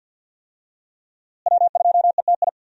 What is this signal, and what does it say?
Morse code, heard as a keyed tone, about 52 wpm, 700 Hz. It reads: W2ETI
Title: USA Call Signs - 5 Characters - Repeated 2x - 50wpm